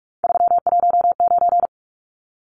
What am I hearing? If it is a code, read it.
Morse code, heard as a keyed tone, about 44 wpm, 700 Hz. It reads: 319
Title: Numbers - 3-Digit 45wpm